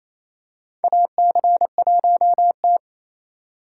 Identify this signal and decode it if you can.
Morse code, heard as a keyed tone, about 28 wpm, 700 Hz. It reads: AC1T
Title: USA Call Signs - 4 Characters - 2x Character Spacing - 28wpm